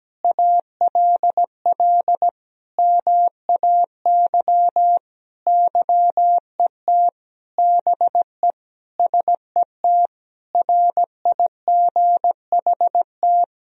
Morse code, heard as a keyed tone, about 17 wpm, 700 Hz. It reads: ALL MAY YET BE SET RIGHT
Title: Sentences from Top 1000 Words - Review Part 1 - 17wpm